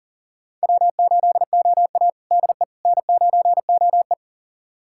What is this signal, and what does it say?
Morse code, heard as a keyed tone, about 40 wpm, 700 Hz. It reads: W8OA DE N9OE
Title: USA Callsign DE Callsign - 4 Characters - 40wpm